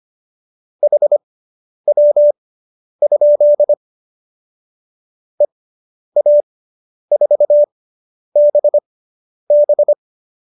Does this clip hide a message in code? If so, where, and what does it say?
Yes: Morse code from 0.8 to 9.9 s, heard as a keyed tone, about 25 wpm, spaced as for 10 wpm, 590 Hz. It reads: HW? EA4BB